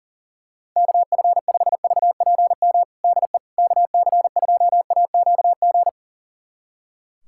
Morse code, heard as a keyed tone, about 40 wpm, 700 Hz. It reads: KF5VPM DE KC2AQG
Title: USA Callsign DE Callsign - 6 Characters - 40wpm